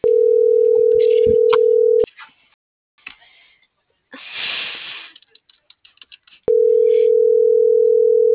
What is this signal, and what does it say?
Ringback tone (North America), 440 + 480 Hz.